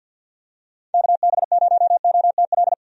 Morse code, heard as a keyed tone, about 46 wpm, 700 Hz. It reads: KB0OTL
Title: USA Call Signs - 6 Characters - 50wpm